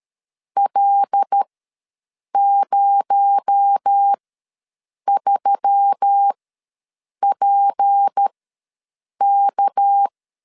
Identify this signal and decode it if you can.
Morse code, heard as a keyed tone, about 13 wpm, spaced as for 7 wpm, 800 Hz. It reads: L03PK
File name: LEZIONI / 6 PKX0 / 6 PKX0 ws 12wpm 3448